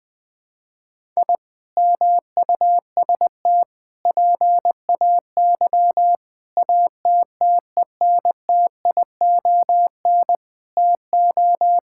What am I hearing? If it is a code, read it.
Morse code, heard as a keyed tone, about 20 wpm, 700 Hz. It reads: I MUST PAY ATTENTION TO
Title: Sentences from Top 1000 Words - Review Part 6 - 20wpm